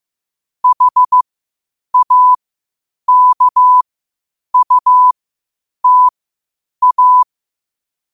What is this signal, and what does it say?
Morse code, heard as a keyed tone, about 15 wpm, spaced as for 8 wpm, 1000 Hz. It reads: HAKUTA
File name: qotdfiles / 8wpm